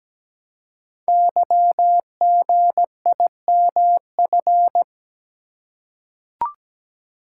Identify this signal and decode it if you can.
Morse code, heard as a keyed tone, about 17 wpm, 700 Hz. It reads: YGIMF
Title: Code Groups - 5 Letter:Numbers - 17wpm